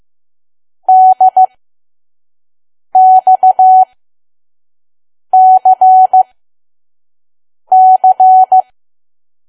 Morse code, heard as a keyed tone, about 15 wpm, spaced as for 5 wpm, 750 Hz. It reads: DXCC
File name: Archive / 5 WPM files / PM